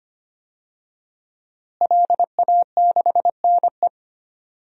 Morse code, heard as a keyed tone, about 25 wpm, 700 Hz. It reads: LA6NE
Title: DXCC-100 Call Signs - 5 Characters - Repeated 3x - 25wpm